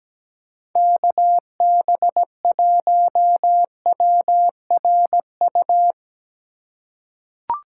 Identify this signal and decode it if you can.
Morse code, heard as a keyed tone, about 17 wpm, 700 Hz. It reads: KB1WRU
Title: USA Call Signs - 6 Characters - 3x Character Spacing - 17wpm